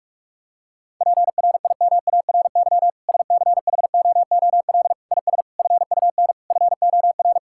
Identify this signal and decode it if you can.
Morse code, heard as a keyed tone, about 46 wpm, 700 Hz. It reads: PRIMARY SCHOOL IS FUN FOR
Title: Sentences from Top 1000 Words - Review Part 5 - 45wpm